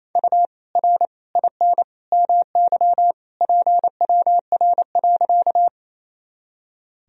Morse code, heard as a keyed tone, about 28 wpm, 700 Hz. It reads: U R IN MY PWR.
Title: Sentences from Top 500 Words and Abbreviations 28wpm